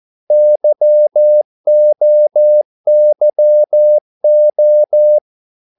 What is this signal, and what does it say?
Morse code, heard as a keyed tone, about 14 wpm, 600 Hz. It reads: YOYO
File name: Korte ord DK 14wpm